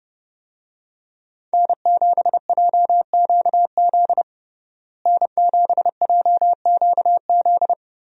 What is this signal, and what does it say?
Morse code, heard as a keyed tone, about 30 wpm, 700 Hz. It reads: N7JQZ N7JQZ
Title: USA Call Signs - 5 Characters - Repeated 3x - 30wpm